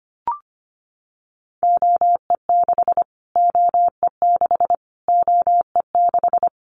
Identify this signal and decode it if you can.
Morse code, heard as a keyed tone, about 25 wpm, 700 Hz. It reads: OE6 OE6 OE6
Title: Callsign Prefixes for Top 25 DXCC - Repeated 3x - 25wpm